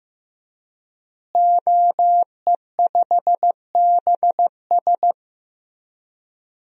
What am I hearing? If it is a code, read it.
Morse code, heard as a keyed tone, about 15 wpm, 700 Hz. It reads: OE5BS
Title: DXCC-100 Call Signs - 5 Characters - Repeated 3x - 15wpm